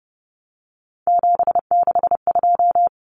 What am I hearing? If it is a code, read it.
Morse code, heard as a keyed tone, about 30 wpm, 700 Hz. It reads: 762
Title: Common Cut Numbers - 3-Digit 30wpm